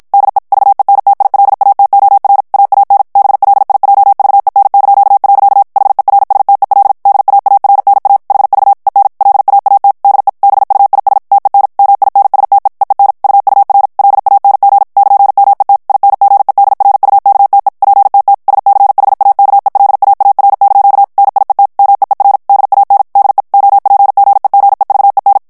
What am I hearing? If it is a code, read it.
Morse code, heard as a keyed tone, about 54 wpm, 800 Hz. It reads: DE VENTILATOR AAN BLIJVEN). HERSTEL DAARNA HV EN LAAT DE BUIS TEN MINSTE EEN UUR LANG <KN>MET INGEDRUKTE PTT SCHAKELAAR) NIET MEER DAN DE OPGEGEVEN
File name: morse mp3 / HOLLANDAIS texte clair 2 vitesse 50 mots minute